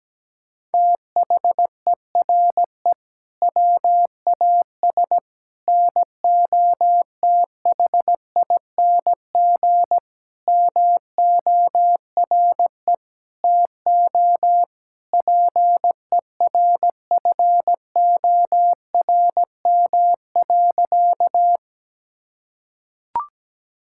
Morse code, heard as a keyed tone, about 17 wpm, 700 Hz. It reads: THERE WAS NOTHING MORE TO PERFORM.
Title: Sentences from Top 1000 Words - Review Part 5 - 17wpm